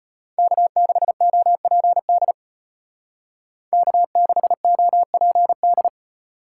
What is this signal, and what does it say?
Morse code, heard as a keyed tone, about 30 wpm, 700 Hz. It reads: K6OPD K6OPD
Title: Speed Racing - USA 1x3 Call Signs 25wpm